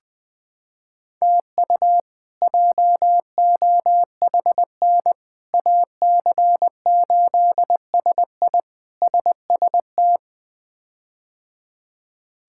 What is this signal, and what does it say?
Morse code, heard as a keyed tone, about 20 wpm, 700 Hz. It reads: TU JOHN AC8SI SST
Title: K1USN SST QSO Practice - 20wpm